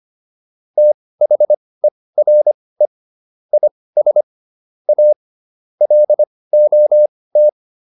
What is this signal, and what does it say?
Morse code, heard as a keyed tone, about 25 wpm, spaced as for 18 wpm, 600 Hz. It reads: THERE IS A LOT